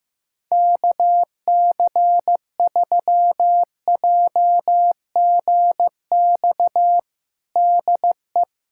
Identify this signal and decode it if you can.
Morse code, heard as a keyed tone, about 15 wpm, 700 Hz. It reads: KC3JGX DE
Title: USA Callsign DE Callsign - 6 Characters - 15wpm